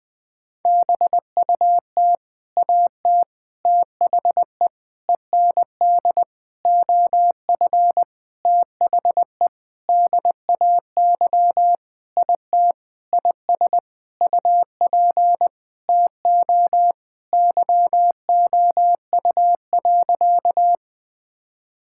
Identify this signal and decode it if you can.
Morse code, heard as a keyed tone, about 20 wpm, 700 Hz. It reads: BUT AT THE END OF THE DAY IT IS UP TO YOU.
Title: Sentences with 7-Word Phrases - Spoken Beginning - 20wpm